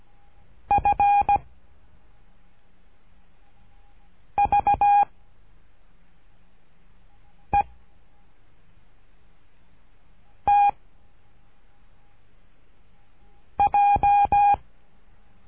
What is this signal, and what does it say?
Morse code, heard as a keyed tone, about 17 wpm, 810 Hz. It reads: F V E T J
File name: cursos / CURSO DE TELEGRAFIA / Telegrafia - Trilha 13